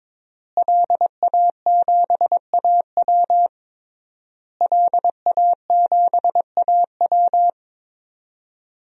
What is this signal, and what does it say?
Morse code, heard as a keyed tone, about 21 wpm, 700 Hz. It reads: LA7AW LA7AW
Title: DXCC-100 Call Signs - 5 Characters - Repeated 3x - 22wpm